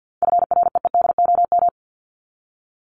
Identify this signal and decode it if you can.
Morse code, heard as a keyed tone, about 50 wpm, 700 Hz. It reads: FREEDOM
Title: Top 501-750 Words 50wpm